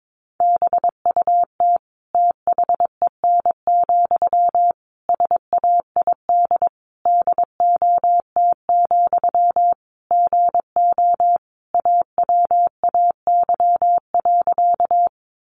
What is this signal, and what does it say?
Morse code, heard as a keyed tone, about 22 wpm, 700 Hz. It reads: BUT THEN, SAID DOT, GO AWAY.
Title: Sentences from Top 1500 Words 22wpm